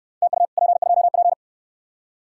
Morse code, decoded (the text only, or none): TU P2P